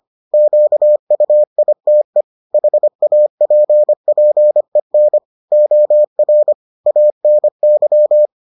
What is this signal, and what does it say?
Morse code, heard as a keyed tone, about 25 wpm, 600 Hz. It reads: QUITE HAPPEN OR ANY